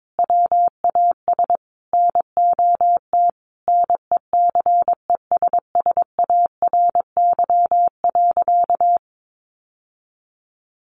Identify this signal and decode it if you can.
Morse code, heard as a keyed tone, about 22 wpm, 700 Hz. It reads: WAS NOT NECESSARY.
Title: Sentences from Top 800 Words 22wpm